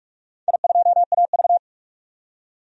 Morse code, heard as a keyed tone, about 46 wpm, 700 Hz. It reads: I2AV
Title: DXCC-100 Call Signs - 4 Characters - 45wpm